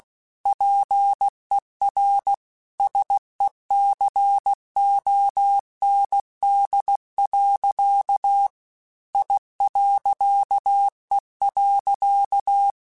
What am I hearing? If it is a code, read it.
Morse code, heard as a keyed tone, about 16 wpm, 800 Hz. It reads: PER SECOND. I.E.
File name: Technical Subjects / Technical Subjects-mp3s / Reactance @15